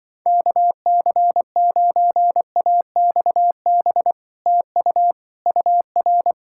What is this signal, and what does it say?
Morse code, heard as a keyed tone, about 24 wpm, 700 Hz. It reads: KC9AXB TU UR
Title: Speed-Racing Common QSO Phrases Encore - 20wpm